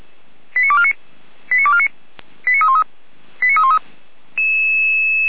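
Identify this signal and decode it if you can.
Five-tone selective call (ZVEI), heard as quick runs of five tones, tones of about 70 ms. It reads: 89249 89249 89313 89313
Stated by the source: ZVEI five-tone call received on an SDR.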